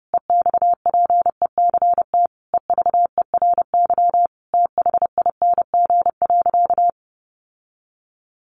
Morse code, heard as a keyed tone, about 30 wpm, 700 Hz. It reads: EXPECT EVERY THING.
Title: Sentences from Top 1100 Words 30wpm